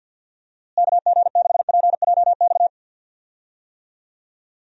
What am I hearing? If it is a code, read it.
Morse code, heard as a keyed tone, about 49 wpm, 700 Hz. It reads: KG6PJX
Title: USA Call Signs - 6 Characters - Repeated 2x - 50wpm